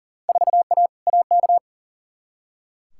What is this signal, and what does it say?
Morse code, heard as a keyed tone, about 40 wpm, 700 Hz. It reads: 4A AK
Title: Field Day Exchanges - 40wpm